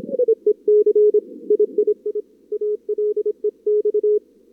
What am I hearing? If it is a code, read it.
Morse code, heard as a keyed tone, about 26 wpm, 440 Hz. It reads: EEC III ALEX
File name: CWSprint / Feb2022 / k5tr-feb-na-sprint-cw-k5ot -0100